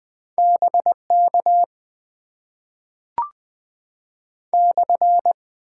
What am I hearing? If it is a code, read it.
Morse code, heard as a keyed tone, about 20 wpm, 700 Hz. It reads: BK /
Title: Top 90 QSO Elements - 8x Character Spacing - 20wpm